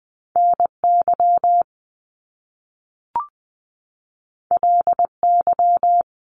Morse code, heard as a keyed tone, about 20 wpm, 700 Hz. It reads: NY LY
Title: Top Two Letter Combinations in Words 20wpm